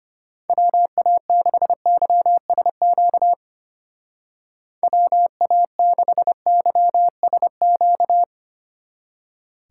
Morse code, heard as a keyed tone, about 33 wpm, 700 Hz. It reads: WA6YSQ WA6YSQ